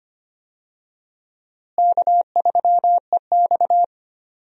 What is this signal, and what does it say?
Morse code, heard as a keyed tone, about 25 wpm, 700 Hz. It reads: K3EX
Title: USA Call Signs - 4 Characters - Repeated 3x - 25wpm